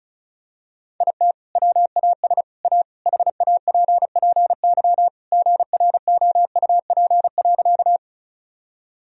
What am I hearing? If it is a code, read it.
Morse code, heard as a keyed tone, about 34 wpm, 700 Hz. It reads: IT WAS A HAPPY GROUP.
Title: Sentences from Top 1000 Words - Review Part 1 - 35wpm